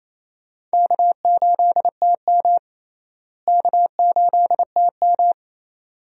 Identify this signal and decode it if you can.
Morse code, heard as a keyed tone, about 28 wpm, 700 Hz. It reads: K8TM K8TM
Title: USA Call Signs - 4 Characters - Repeated 2x - 28wpm